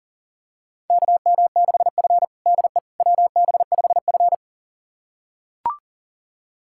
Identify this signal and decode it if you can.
Morse code, heard as a keyed tone, about 40 wpm, 700 Hz. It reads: KM6F DE WB5F
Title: USA Callsign DE Callsign - 4 Characters - 40wpm